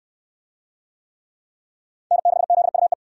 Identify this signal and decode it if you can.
Morse code, heard as a keyed tone, about 60 wpm, 700 Hz. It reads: N6ZUE